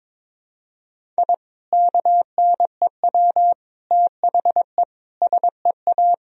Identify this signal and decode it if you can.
Morse code, heard as a keyed tone, about 22 wpm, 700 Hz. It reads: I KNEW THE SEA
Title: Sentences from Top 1000 Words - Review Part 1 - 22wpm